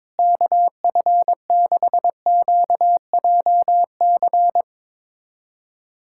Morse code, heard as a keyed tone, about 22 wpm, 700 Hz. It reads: KF6QJC
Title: USA Callsign DE Callsign 22wpm